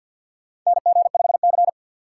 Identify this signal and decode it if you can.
Morse code, heard as a keyed tone, about 51 wpm, 700 Hz. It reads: NG5C
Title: USA Call Signs - 4 Characters - 3x Character Spacing - 50wpm